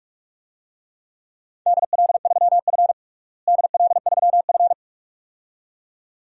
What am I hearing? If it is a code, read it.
Morse code, heard as a keyed tone, about 46 wpm, 700 Hz. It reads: DL3F DL3F
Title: DXCC-100 Call Signs - 4 Characters - Repeated 2x - 45wpm